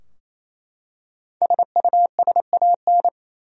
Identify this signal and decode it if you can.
Morse code, heard as a keyed tone, about 27 wpm, 700 Hz. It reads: SUSAN
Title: USA Common Names - 28wpm